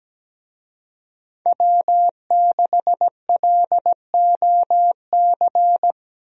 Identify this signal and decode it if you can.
Morse code, heard as a keyed tone, about 17 wpm, 700 Hz. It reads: W6LOC
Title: USA Call Signs - 5 Characters - Repeated 2x - 17wpm